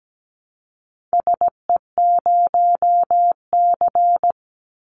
Morse code, heard as a keyed tone, about 17 wpm, 700 Hz. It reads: SE0C
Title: Call Signs - 17wpm